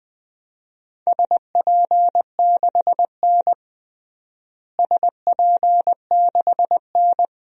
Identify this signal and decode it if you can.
Morse code, heard as a keyed tone, about 20 wpm, 700 Hz. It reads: SP6N SP6N
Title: DXCC-100 Call Signs - 4 Characters - Repeated 2x - 20wpm